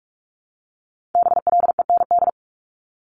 Morse code, heard as a keyed tone, about 46 wpm, 700 Hz. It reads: BLEND
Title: Multi-letter Phonemes 45wpm